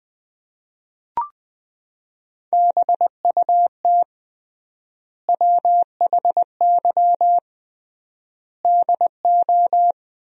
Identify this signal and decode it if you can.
Morse code, heard as a keyed tone, about 20 wpm, 700 Hz. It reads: BUT WHY DO
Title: 3x Word-Spacing - Sentences from Top 200 Words - 20wpm